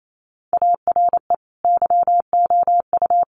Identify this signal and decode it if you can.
Morse code, heard as a keyed tone, about 28 wpm, 700 Hz. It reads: ARE YOU
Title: Sentences from Top 1200 Words 28wpm